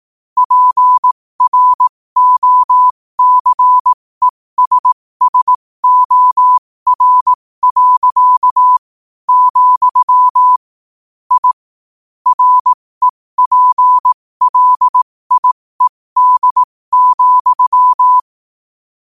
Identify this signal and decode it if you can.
Morse code, heard as a keyed tone, about 18 wpm, spaced as for 15 wpm, 1000 Hz. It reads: PROCESSOR. , I REPLIED,
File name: qotdfiles / 15wpm